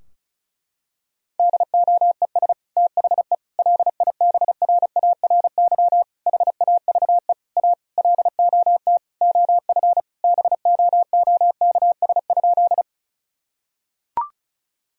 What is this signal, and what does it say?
Morse code, heard as a keyed tone, about 34 wpm, 700 Hz. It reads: DOES THE LIBRARY HAVE A LOT OF BOOKS?
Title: Sentences from Top 1000 Words - Review Part 4 - 35wpm